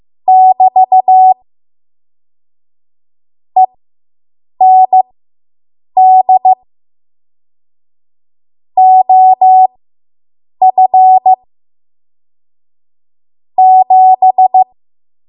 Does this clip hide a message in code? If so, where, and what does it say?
Yes: Morse code from 0.3 to 14.6 s, heard as a keyed tone, about 15 wpm, spaced as for 7 wpm, 750 Hz. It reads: <BT> END OF 7